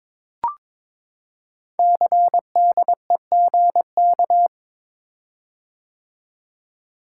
Morse code, heard as a keyed tone, about 22 wpm, 700 Hz. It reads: CDEGK
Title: Code Groups - 5 Letters - 22wpm